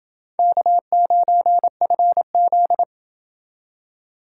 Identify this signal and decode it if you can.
Morse code, heard as a keyed tone, about 29 wpm, 700 Hz. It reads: K9FZ